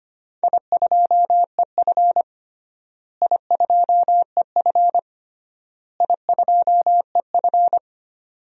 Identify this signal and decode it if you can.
Morse code, heard as a keyed tone, about 25 wpm, 700 Hz. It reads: I2EF I2EF I2EF
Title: DXCC-100 Call Signs - 4 Characters - Repeated 3x - 25wpm